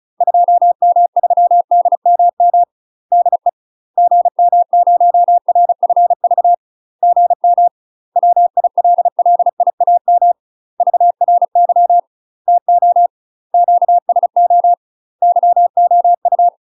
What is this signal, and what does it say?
Morse code, heard as a keyed tone, about 36 wpm, 700 Hz. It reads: 2M3DMM DE GM0RFV GM WILLIAM VRY TO QSO YOU
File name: UK MASTER QSOs TRAINING / 35WPM / UK MASTER QSOs 0009